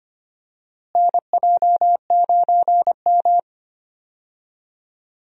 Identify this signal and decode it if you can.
Morse code, heard as a keyed tone, about 25 wpm, 700 Hz. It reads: NJ9M
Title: USA Call Signs - 4 Characters - Repeated 3x - 25wpm